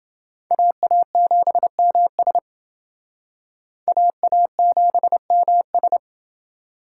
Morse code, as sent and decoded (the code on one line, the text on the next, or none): .- .- --... -- ... / .- .- --... -- ...
AA7MS AA7MS